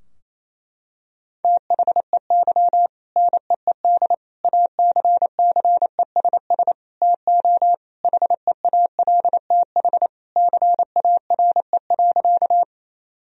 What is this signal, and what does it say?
Morse code, heard as a keyed tone, about 28 wpm, 700 Hz. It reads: THEY NEED ACCESS TO HEALTH CARE.
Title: Sentences from Top 1000 Words - Review Part 4 - 28wpm